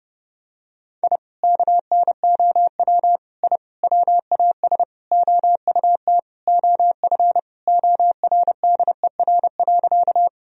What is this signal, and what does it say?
Morse code, heard as a keyed tone, about 30 wpm, 700 Hz. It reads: I KNOW I WAS OUT OF ORDER.
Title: Sentences from Top 1000 Words - Review Part 2 - 30wpm